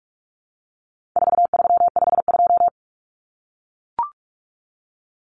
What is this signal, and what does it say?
Morse code, heard as a keyed tone, about 44 wpm, 700 Hz. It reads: 4352
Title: Common Cut Numbers - 4-Digit 45wpm